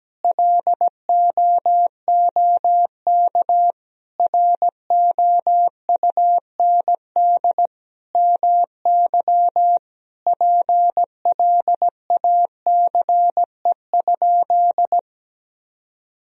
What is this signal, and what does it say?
Morse code, heard as a keyed tone, about 17 wpm, 700 Hz. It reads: LOOK ROUND MY PLACE?
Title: Sentences from Top 200 Words and Abbreviations 17wpm